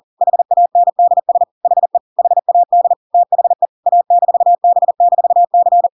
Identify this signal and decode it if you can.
Morse code, heard as a keyed tone, about 40 wpm, 700 Hz. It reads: HANDS HE HAD THE A-B-C
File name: Pinocchio / Pinocchio ch8 40 40 ik1hge